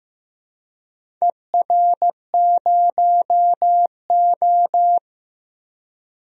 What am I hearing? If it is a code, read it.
Morse code, heard as a keyed tone, about 15 wpm, 700 Hz. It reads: ER0O
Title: DXCC-100 Call Signs - 4 Characters - 15wpm